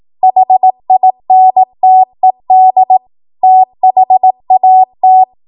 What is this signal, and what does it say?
Morse code, heard as a keyed tone, about 18 wpm, 750 Hz. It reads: HINTED THAT
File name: Morse / 18WPM